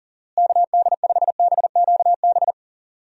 Morse code, heard as a keyed tone, about 40 wpm, 700 Hz. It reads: KD5BQB
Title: USA Call Signs - 6 Characters - Repeated 2x - 40wpm